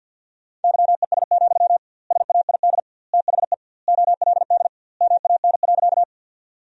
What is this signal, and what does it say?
Morse code, heard as a keyed tone, about 51 wpm, 700 Hz. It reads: YES, SAID THE OLD MAN.
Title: 2x Word-Spacing - Sentences from Top 500 Words - 50wpm